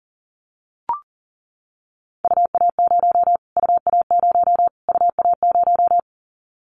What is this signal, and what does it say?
Morse code, heard as a keyed tone, about 37 wpm, 700 Hz. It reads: UA0 UA0 UA0
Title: Callsign Prefixes for Top 25 DXCC - Repeated 3x - 40wpm